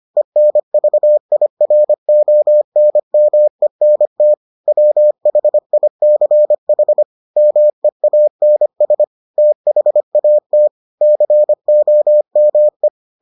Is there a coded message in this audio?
Yes: Morse code from 0.2 to 12.9 s, heard as a keyed tone, about 25 wpm, 600 Hz. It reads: ENVIRONMENT WHICH MEANS THAT COME